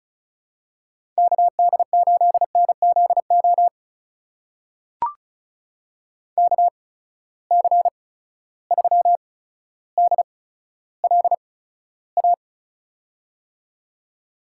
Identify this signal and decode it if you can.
Morse code, heard as a keyed tone, about 34 wpm, spaced as for 15 wpm, 700 Hz. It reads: KD8NZO KC3DLA
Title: USA Call Signs - 6 Characters - 8x Character Spacing - 35wpm